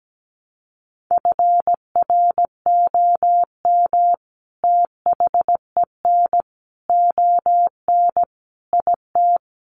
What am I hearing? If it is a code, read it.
Morse code, heard as a keyed tone, about 17 wpm, 700 Hz. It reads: FROM THEN ON IT
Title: Sentences from Top 400 Words 17wpm